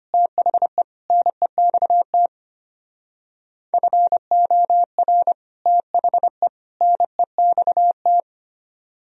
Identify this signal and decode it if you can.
Morse code, heard as a keyed tone, about 32 wpm, 700 Hz. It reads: THE NEXT FOR THE NEXT